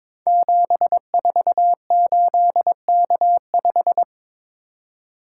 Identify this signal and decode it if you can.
Morse code, heard as a keyed tone, about 22 wpm, 700 Hz. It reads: 748K5
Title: Code Groups - 5 Letter:Numbers - 22wpm